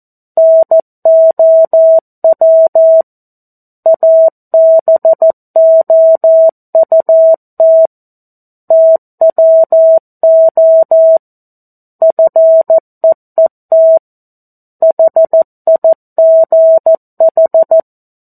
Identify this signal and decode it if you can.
Morse code, heard as a keyed tone, about 14 wpm, 650 Hz. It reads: NOW ABOUT TWO FEET HIGH
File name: 02-Chapter-02 14 WPM